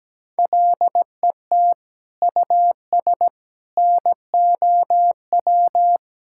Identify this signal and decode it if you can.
Morse code, heard as a keyed tone, about 17 wpm, 700 Hz. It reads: LET US NOW
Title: Sentences from Top 1000 Words - Review Part 6 - 17wpm